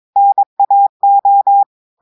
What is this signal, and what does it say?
Morse code, heard as a keyed tone, about 23 wpm, 800 Hz. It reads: NAO
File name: audios / portugais / portugais texte en clair 2 vitesse 22 mots minutes